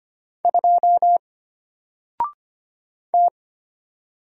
Morse code, heard as a keyed tone, about 25 wpm, 700 Hz. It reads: 2 T
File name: Single-Letter-Number-25wpm